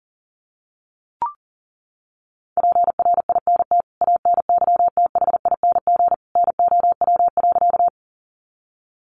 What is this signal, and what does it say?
Morse code, heard as a keyed tone, about 40 wpm, 700 Hz. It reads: PRINT ANYTHING NOW.
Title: Sentences from Top 1400 Words 40wpm